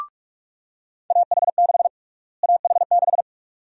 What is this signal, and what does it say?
Morse code, heard as a keyed tone, about 46 wpm, 700 Hz. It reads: AH6 AH6
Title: USA Call Sign Prefixes - Repeated 2x - 45wpm